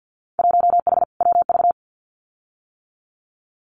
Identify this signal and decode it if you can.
Morse code, heard as a keyed tone, about 50 wpm, 700 Hz. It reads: 1H WV
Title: Winter Field Day Exchanges Encore 50wpm